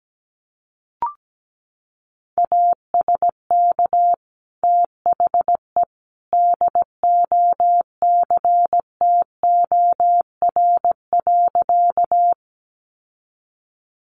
Morse code, heard as a keyed tone, about 17 wpm, 700 Hz. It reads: ASK THE DOCTOR.
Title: Sentences from Top 1500 Words 17wpm